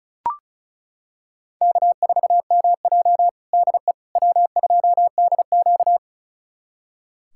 Morse code, heard as a keyed tone, about 34 wpm, 700 Hz. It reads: K4MJ DE W2DQ
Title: USA Callsign DE Callsign - 4 Characters - 35wpm